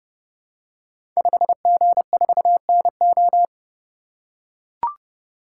Morse code, heard as a keyed tone, about 29 wpm, 700 Hz. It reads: 5G4NO
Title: Code Groups - 5 Letter:Numbers - 30wpm